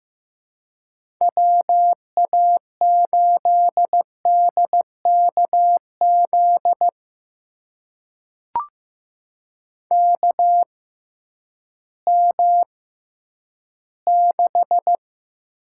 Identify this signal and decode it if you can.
Morse code, heard as a keyed tone, about 15 wpm, spaced as for 10 wpm, 700 Hz. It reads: WA8DKZ KM6
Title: USA Call Signs - 6 Characters - 6x Character Spacing - 15wpm